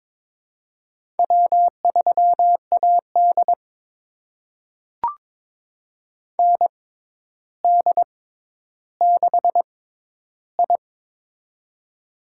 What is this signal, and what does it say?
Morse code, heard as a keyed tone, about 22 wpm, spaced as for 11 wpm, 700 Hz. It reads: W3AD ND6I
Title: USA Call Signs - 4 Characters - 6x Character Spacing - 22wpm